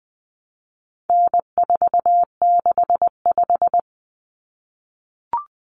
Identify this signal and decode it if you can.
Morse code, heard as a keyed tone, about 20 wpm, 700 Hz. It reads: N465
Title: Common Cut Numbers - 4-Digit 20wpm